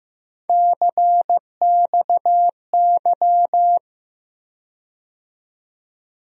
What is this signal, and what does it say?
Morse code, heard as a keyed tone, about 15 wpm, 700 Hz. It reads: CXY
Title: USA Call Sign Postfixes - 3 Letters 15wpm